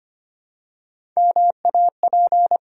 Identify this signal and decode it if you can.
Morse code, heard as a keyed tone, about 25 wpm, 700 Hz. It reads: MAP
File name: practice / 25wpm / Easy Words with Hints - 25wpm